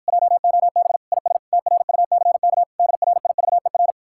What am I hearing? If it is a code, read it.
Morse code, heard as a keyed tone, about 53 wpm, 700 Hz. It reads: JOB IS TRUCK DRIVER